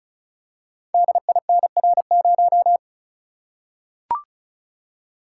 Morse code, heard as a keyed tone, about 36 wpm, 690 Hz. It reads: DINR0
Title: Code Groups - 5 Letter:Numbers - 35wpm